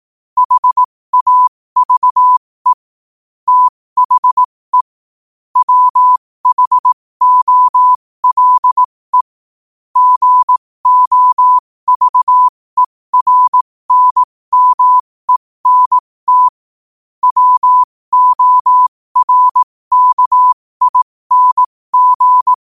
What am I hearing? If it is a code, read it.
Morse code, heard as a keyed tone, about 18 wpm, spaced as for 15 wpm, 1000 Hz. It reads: HAVE THE WHOLE GOVERNMENT WORKING
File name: qotdfiles / 15wpm